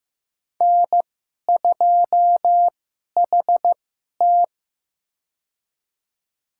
Morse code, heard as a keyed tone, about 15 wpm, spaced as for 11 wpm, 700 Hz. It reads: N2HT
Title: USA Call Signs - 4 Characters - 2x Character Spacing - 15wpm